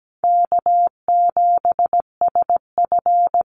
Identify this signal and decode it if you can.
Morse code, heard as a keyed tone, about 17 wpm, 700 Hz. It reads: K7SF
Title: Easy Call Signs Repeated 2x - 17wpm